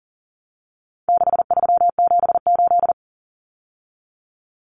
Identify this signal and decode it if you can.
Morse code, heard as a keyed tone, about 40 wpm, 700 Hz. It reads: 6378
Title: Numbers - 4-Digit 40wpm